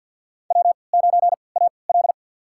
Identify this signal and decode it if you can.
Morse code, heard as a keyed tone, about 52 wpm, spaced as for 29 wpm, 700 Hz. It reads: W9AL
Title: USA Call Signs - 4 Characters - 3x Character Spacing - 50wpm